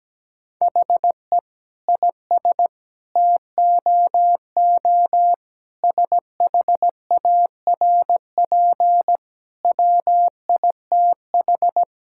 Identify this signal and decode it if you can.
Morse code, heard as a keyed tone, about 17 wpm, 700 Hz. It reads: HE IS TOO SHARP WITH
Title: Sentences from Top 1600 Words 17wpm